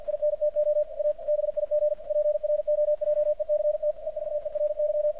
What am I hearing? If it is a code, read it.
Morse code, heard as a keyed tone, about 48 wpm, 600 Hz, with radio noise. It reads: INTO A BIG PROJECT EVERY
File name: archived / CRSnet